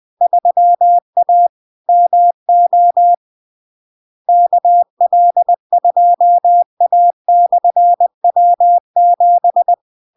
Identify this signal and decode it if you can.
Morse code, heard as a keyed tone, about 20 wpm, 700 Hz. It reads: 3A MO KL2A/W7
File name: LARK-FD-2022-Log-in-CW-20wpm-700-Hz